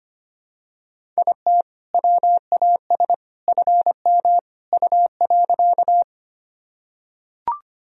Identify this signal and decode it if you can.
Morse code, heard as a keyed tone, about 25 wpm, 700 Hz. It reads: IT WAS FM U.
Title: Sentences from Top 100 Words and Abbreviations 25wpm